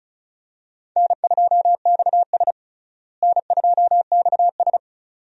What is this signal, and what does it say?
Morse code, heard as a keyed tone, about 36 wpm, 700 Hz. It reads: N2XS N2XS
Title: USA Call Signs - 4 Characters - Repeated 2x - 35wpm